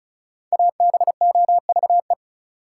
Morse code, heard as a keyed tone, about 36 wpm, 700 Hz. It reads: ABOVE
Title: Sentences from Top 1000 Words - Review Part 2 - New Word Drill - 35wpm